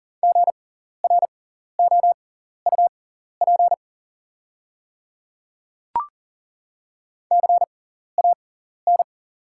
Morse code, heard as a keyed tone, about 41 wpm, spaced as for 14 wpm, 690 Hz. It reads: GROUP CAN
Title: 6x Character Spacing - Top 300 Words - 40wpm